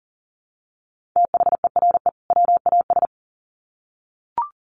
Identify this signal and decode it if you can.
Morse code, heard as a keyed tone, about 40 wpm, 700 Hz. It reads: THERE WAS
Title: Sets of 2 Words 40wpm